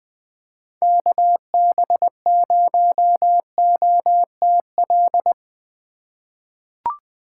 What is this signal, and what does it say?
Morse code, heard as a keyed tone, about 21 wpm, 700 Hz. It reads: KB0OTL
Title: USA Call Signs - 4x Character Spacing - 20wpm